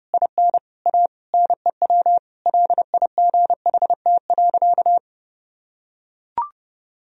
Morse code, heard as a keyed tone, about 30 wpm, 700 Hz. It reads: IN A NEW LIGHT.
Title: Sentences from Top 1000 Words - Review Part 1 - 30wpm